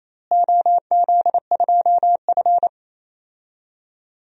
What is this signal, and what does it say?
Morse code, heard as a keyed tone, about 28 wpm, 700 Hz. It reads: OZ2F
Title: DXCC-100 Call Signs - 4 Characters - Repeated 3x - 28wpm